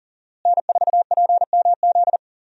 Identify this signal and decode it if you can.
Morse code, heard as a keyed tone, about 40 wpm, 700 Hz. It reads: N4PMZ
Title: USA Call Signs - 5 Characters - Repeated 3x - 40wpm